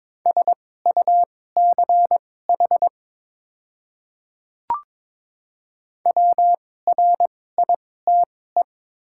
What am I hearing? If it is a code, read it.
Morse code, heard as a keyed tone, about 22 wpm, spaced as for 16 wpm, 700 Hz. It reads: SUCH WRITE
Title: 2x Character Spacing - Top 200 Words - 22wpm